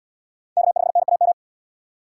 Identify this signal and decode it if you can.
Morse code, heard as a keyed tone, about 64 wpm, 700 Hz. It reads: K5NAM